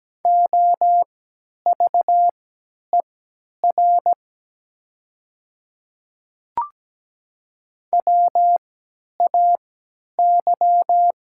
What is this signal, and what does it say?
Morse code, heard as a keyed tone, about 17 wpm, spaced as for 10 wpm, 700 Hz. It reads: OVER WAY
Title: 3x Character Spacing - Top 100 Words - 17wpm